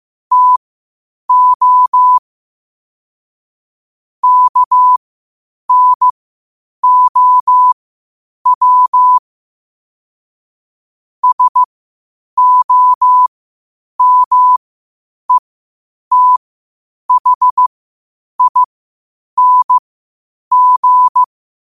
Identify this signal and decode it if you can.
Morse code, heard as a keyed tone, about 15 wpm, spaced as for 8 wpm, 1000 Hz. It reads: TO KNOW SOMETHING
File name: qotdfiles / 8wpm